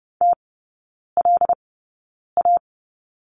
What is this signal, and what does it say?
Morse code, heard as a keyed tone, about 30 wpm, spaced as for 9 wpm, 700 Hz. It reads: TLA
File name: 081 Lesson 14e - Number 2 - Three Characters 30wpm